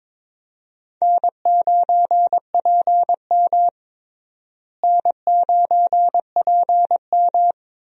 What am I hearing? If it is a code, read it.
Morse code, heard as a keyed tone, about 23 wpm, 700 Hz. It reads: N9PM N9PM
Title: USA Call Signs - 4 Characters - Repeated 2x - 22wpm